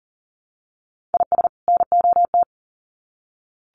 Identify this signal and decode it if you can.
Morse code, heard as a keyed tone, about 39 wpm, 700 Hz. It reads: IS NOT
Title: Sets of 2 Words 40wpm